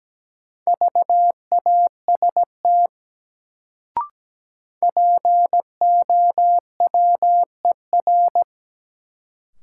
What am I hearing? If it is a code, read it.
Morse code, heard as a keyed tone, about 17 wpm, 700 Hz. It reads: VAST POWER
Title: Sets of 4 Words - New Word Drill - 17wpm